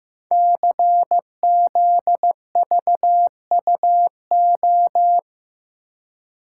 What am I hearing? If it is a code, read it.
Morse code, heard as a keyed tone, about 15 wpm, 700 Hz. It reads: CZVUO
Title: Code Groups - 5 Letter:Numbers - 15wpm